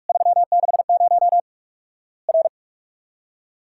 Morse code, heard as a keyed tone, about 48 wpm, 700 Hz. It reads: 360 R
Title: Worked All Europe DX Contest - QTC Exchange - 45wpm